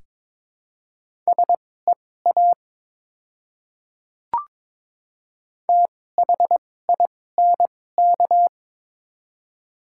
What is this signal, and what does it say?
Morse code, heard as a keyed tone, about 22 wpm, spaced as for 16 wpm, 700 Hz. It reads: SEA THINK
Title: 2x Character Spacing - Top 500 Words - 22wpm